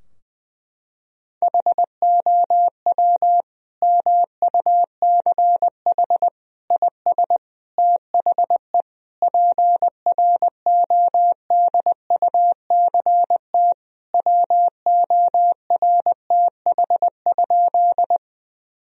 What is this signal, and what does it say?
Morse code, heard as a keyed tone, about 20 wpm, 700 Hz. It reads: HOW MUCH IS THE PRODUCT WORTH?
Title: Sentences from Top 1000 Words - Review Part 2 - 20wpm